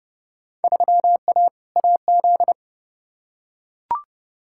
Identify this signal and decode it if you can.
Morse code, heard as a keyed tone, about 29 wpm, 700 Hz. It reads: 3A AZ
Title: Field Day Exchanges - 30wpm